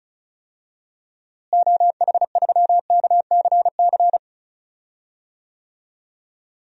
Morse code, heard as a keyed tone, about 34 wpm, 700 Hz. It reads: OH3KCC
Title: DXCC-100 Call Signs - 6 Characters - Repeated 3x - 35wpm